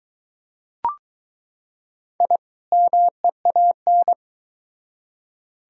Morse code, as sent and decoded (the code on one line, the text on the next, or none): .. / -- . .- -.
I MEAN